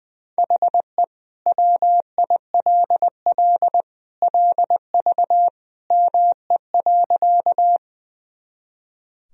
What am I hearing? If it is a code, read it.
Morse code, heard as a keyed tone, about 20 wpm, 700 Hz. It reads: HE WILL LV ME.
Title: Sentences from Top 400 Words and Abbreviations 20wpm